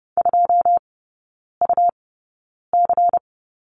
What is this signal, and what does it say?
Morse code, heard as a keyed tone, about 30 wpm, spaced as for 9 wpm, 700 Hz. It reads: 2UC
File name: Lesson 19e - Letter w - Three Characters 30wpm